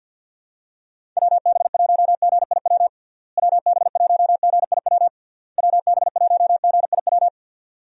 Morse code, heard as a keyed tone, about 49 wpm, 700 Hz. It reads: WB1GIW WB1GIW WB1GIW
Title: USA Call Signs - 6 Characters - Repeated 3x - 50wpm